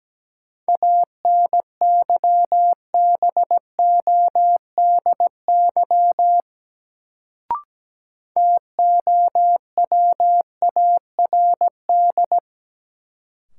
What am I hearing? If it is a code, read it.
Morse code, heard as a keyed tone, about 17 wpm, 700 Hz. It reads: ANYBODY TOWARD
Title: Sets of 5 Words - New Word Drill - 17wpm